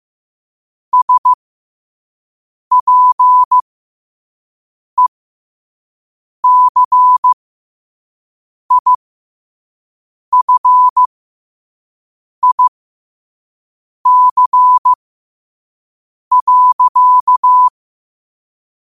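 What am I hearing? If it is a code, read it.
Morse code, heard as a keyed tone, about 15 wpm, spaced as for 5 wpm, 1000 Hz. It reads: SPECIFIC.
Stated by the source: Morse sent at 5 wpm.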